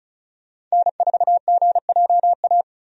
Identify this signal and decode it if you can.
Morse code, heard as a keyed tone, about 36 wpm, 700 Hz. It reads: N4GJA
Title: USA Call Signs - 5 Characters - 35wpm